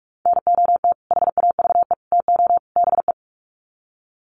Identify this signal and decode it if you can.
Morse code, heard as a keyed tone, about 44 wpm, 700 Hz. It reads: NOT HAVE TO BE
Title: Sets of 4 Words 45wpm